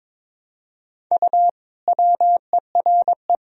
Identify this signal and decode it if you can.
Morse code, heard as a keyed tone, about 21 wpm, 700 Hz. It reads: U WERE
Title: Sentences from Top 400 Words and Abbreviations 22wpm